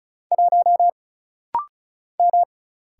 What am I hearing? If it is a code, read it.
Morse code, heard as a keyed tone, about 36 wpm, 700 Hz. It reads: 1 M
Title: Single Letter-Number - Rapid-Fire 35wpm